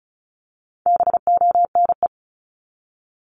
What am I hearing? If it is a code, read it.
Morse code, heard as a keyed tone, about 34 wpm, 700 Hz. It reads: BONE
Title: Top 1501-1750 Words 35wpm